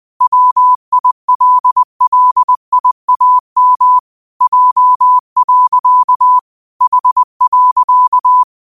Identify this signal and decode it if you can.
Morse code, heard as a keyed tone, about 20 wpm, 1000 Hz. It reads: WILLIAM J. H.
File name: qotdfiles / 20wpm